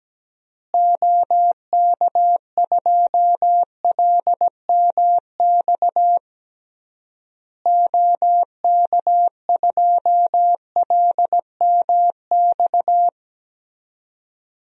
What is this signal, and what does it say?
Morse code, heard as a keyed tone, about 17 wpm, 700 Hz. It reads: OK2LMX OK2LMX
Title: DXCC-100 Call Signs - 6 Characters - Repeated 3x - 17wpm